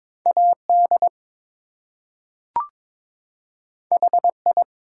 Morse code, heard as a keyed tone, about 22 wpm, 700 Hz. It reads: AD HI
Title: Two Letter Words - 22wpm